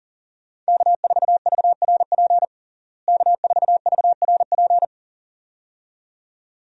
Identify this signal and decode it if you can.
Morse code, heard as a keyed tone, about 40 wpm, 700 Hz. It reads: K4VRP K4VRP
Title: USA Call Signs - 5 Characters - Repeated 2x - 40wpm